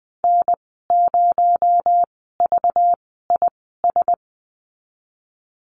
Morse code, heard as a keyed tone, about 20 wpm, spaced as for 15 wpm, 700 Hz. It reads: N0VIS
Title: 231 Lesson 37g - Letter x - Call Signs 20wpm